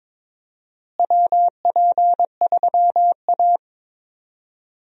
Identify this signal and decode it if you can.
Morse code, heard as a keyed tone, about 22 wpm, 700 Hz. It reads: WP3A
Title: USA Call Signs - 4 Characters - Repeated 2x - 22wpm